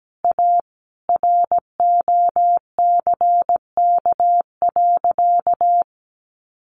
Morse code, heard as a keyed tone, about 17 wpm, 700 Hz. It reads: A ROCK.
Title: Sentences from Top 1500 Words 17wpm